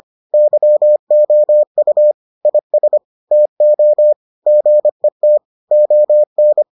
Morse code, heard as a keyed tone, about 25 wpm, 600 Hz. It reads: YOU IS TO GET ON